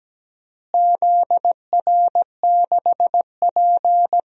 Morse code, heard as a keyed tone, about 17 wpm, 700 Hz. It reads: ZR6P
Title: DXCC-100 Call Signs - 4 Characters - Repeated 3x - 17wpm